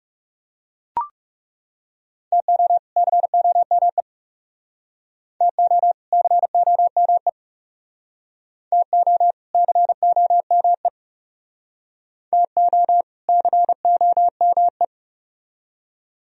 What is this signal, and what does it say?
Morse code, heard as a keyed tone, about 36 wpm, 700 Hz. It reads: TO COME TO COME TO COME TO COME
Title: Speed-Racing - Sets of 2 Words 30wpm